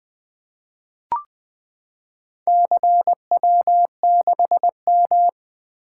Morse code, heard as a keyed tone, about 20 wpm, 700 Hz. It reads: CW6M
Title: Four Letter-Number Combinations - 20wpm